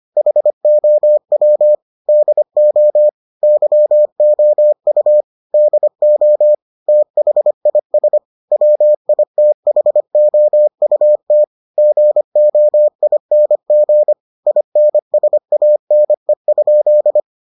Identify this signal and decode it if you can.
Morse code, heard as a keyed tone, about 25 wpm, 600 Hz. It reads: HOW DO YOU DO THIS WITHOUT GOING INSANE?